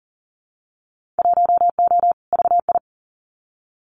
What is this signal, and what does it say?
Morse code, heard as a keyed tone, about 42 wpm, 700 Hz. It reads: 1O VI
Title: Winter Field Day Exchanges Encore 40wpm